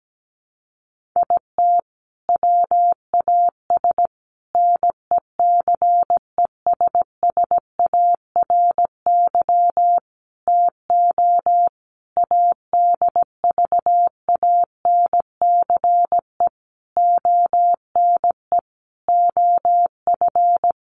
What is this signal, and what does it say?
Morse code, heard as a keyed tone, about 17 wpm, 700 Hz. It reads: IT WAS NECESSARY TO ADVANCE ONE OF
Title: Sentences from Top 1400 Words 17wpm